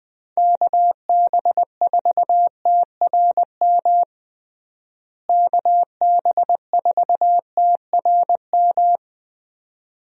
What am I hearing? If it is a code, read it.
Morse code, heard as a keyed tone, about 21 wpm, 700 Hz. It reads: KB4TRM KB4TRM
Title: USA Call Signs - 6 Characters - Repeated 2x - 20wpm